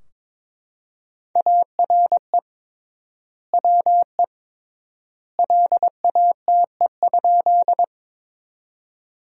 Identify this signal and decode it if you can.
Morse code, heard as a keyed tone, about 22 wpm, 700 Hz. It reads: ARE WE LATE?
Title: 3x Word-Spacing - Sentences from Top 300 Words - 22wpm